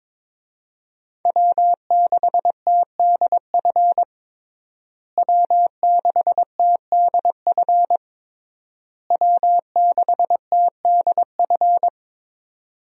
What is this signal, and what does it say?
Morse code, heard as a keyed tone, about 22 wpm, 700 Hz. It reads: W6TDF W6TDF W6TDF
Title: USA Call Signs - 5 Characters - Repeated 3x - 22wpm